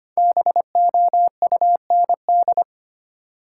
Morse code, heard as a keyed tone, about 25 wpm, 700 Hz. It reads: BOUND
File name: 5 Letter Words - First 3 Letters Given - 25wpm